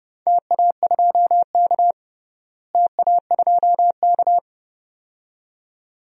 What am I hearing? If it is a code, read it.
Morse code, heard as a keyed tone, about 30 wpm, 700 Hz. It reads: TA2K TA2K
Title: DXCC-100 Call Signs - 4 Characters - Repeated 2x - 30wpm